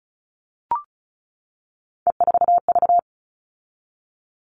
Morse code, heard as a keyed tone, about 34 wpm, 700 Hz. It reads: E4V
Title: All Cut Numbers - 3-Digit 35wpm